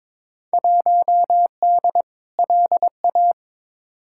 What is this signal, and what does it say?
Morse code, heard as a keyed tone, about 22 wpm, 700 Hz. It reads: 1D LA
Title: Field Day Exchanges - 22wpm